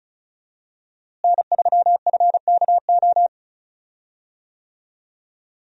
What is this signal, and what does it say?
Morse code, heard as a keyed tone, about 36 wpm, 700 Hz. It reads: N3FKO
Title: USA Call Signs - 5 Characters - Repeated 2x - 35wpm